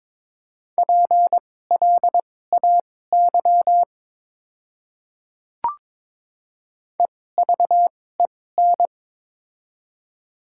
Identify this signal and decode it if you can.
Morse code, heard as a keyed tone, about 22 wpm, spaced as for 16 wpm, 700 Hz. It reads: PLAY EVEN
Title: 2x Character Spacing - Top 300 Words - 22wpm